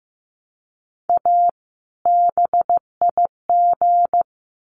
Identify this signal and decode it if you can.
Morse code, heard as a keyed tone, about 15 wpm, 700 Hz. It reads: A BIG
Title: Sets of 2 Words 15wpm